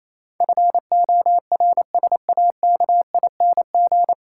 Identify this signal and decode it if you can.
Morse code, heard as a keyed tone, about 28 wpm, 700 Hz. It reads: FORSAKING
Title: Pangram Challenge v2 - 28wpm